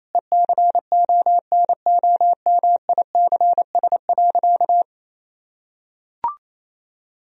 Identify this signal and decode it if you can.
Morse code, heard as a keyed tone, about 28 wpm, 700 Hz. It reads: ECONOMICS.
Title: Sentences from Top 1000 Words - Review Part 3 - 28wpm